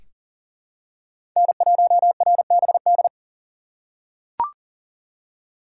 Morse code, heard as a keyed tone, about 40 wpm, 700 Hz. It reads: N1RBD
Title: K1USN SST QSO Practice - 40wpm